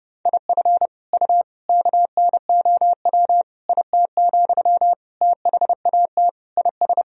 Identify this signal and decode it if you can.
Morse code, heard as a keyed tone, about 32 wpm, 700 Hz. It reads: IF U KNOW IT, THAT IS
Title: Sentences from Top 400 Words and Abbreviations 30wpm